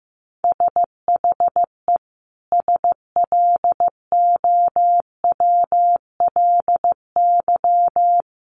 Morse code, heard as a keyed tone, about 15 wpm, 700 Hz. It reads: SHE SLOWLY
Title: Sentences from Top 1100 Words 15wpm